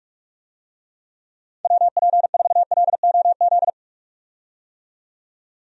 Morse code, heard as a keyed tone, about 46 wpm, 700 Hz. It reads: WP4LOZ
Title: USA Call Signs - 6 Characters - 45wpm